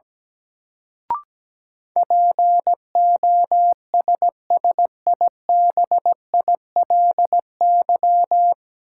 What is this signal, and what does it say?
Morse code, heard as a keyed tone, about 17 wpm, 700 Hz. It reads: POSSIBILY
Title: Sentences from Top 1300 Words - New Word Drill - 17wpm